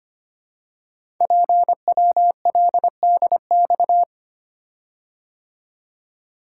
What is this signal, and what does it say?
Morse code, heard as a keyed tone, about 25 wpm, 700 Hz. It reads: PWLDX
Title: Code Groups - 5 Letter:Numbers - 25wpm